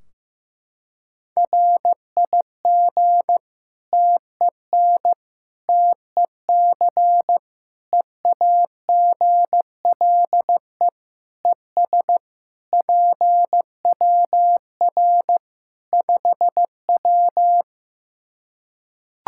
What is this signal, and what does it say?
Morse code, heard as a keyed tone, about 15 wpm, 700 Hz. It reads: RIG TEN TEC EAGLE ES PWR 5W
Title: Practice QSO - Exchange 2 - Info - 15wpm